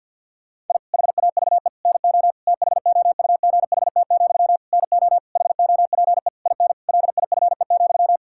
Morse code, heard as a keyed tone, about 52 wpm, 700 Hz. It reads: I HAVE NO THOUGHT, NO HOPE IN LIFE,
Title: Sentences from Top 1000 Words - Review Part 6 - 50wpm